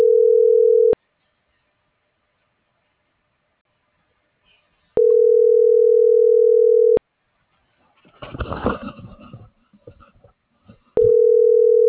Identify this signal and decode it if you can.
Ringback tone (North America), 440 + 480 Hz.
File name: CENTRAL SOTOVA / out carlos.sotova